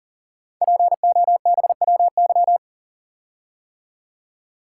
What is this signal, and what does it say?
Morse code, heard as a keyed tone, about 38 wpm, 690 Hz. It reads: POBWY
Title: Code Groups - 5 Letters - 40wpm